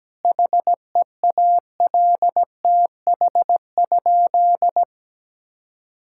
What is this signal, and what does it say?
Morse code, heard as a keyed tone, about 17 wpm, 700 Hz. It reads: HEALTH?
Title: Sentences from Top 1000 Words - Review Part 5 - 17wpm